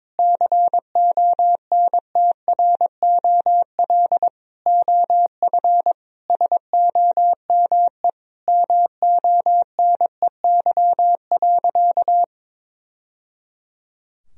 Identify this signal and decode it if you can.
Morse code, heard as a keyed tone, about 22 wpm, 700 Hz. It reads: CONTROL OF SOME MONEY.
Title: Sentences from Top 1000 Words - Review Part 3 - 22wpm